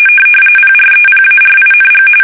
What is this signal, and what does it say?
RTTY, about 45.45 baud. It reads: PREPARING
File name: fsk850